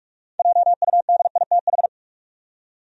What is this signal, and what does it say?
Morse code, heard as a keyed tone, about 46 wpm, 700 Hz. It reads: JUDITH
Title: USA Common Names - 45wpm